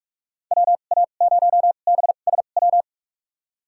Morse code, heard as a keyed tone, about 42 wpm, spaced as for 31 wpm, 700 Hz. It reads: WA0BSW
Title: USA Call Signs - 2x Character Spacing - 45wpm